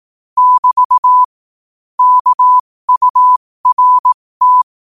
Morse code, heard as a keyed tone, about 18 wpm, spaced as for 15 wpm, 1000 Hz. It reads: <BT> KURT
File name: qotdfiles / 15wpm